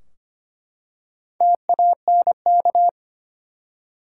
Morse code, heard as a keyed tone, about 25 wpm, 700 Hz. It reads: TANK
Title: Four Letter Words - 25wpm